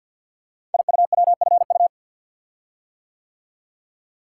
Morse code, heard as a keyed tone, about 48 wpm, 700 Hz. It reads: IUWFU
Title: Code Groups - 5 Letters - 50wpm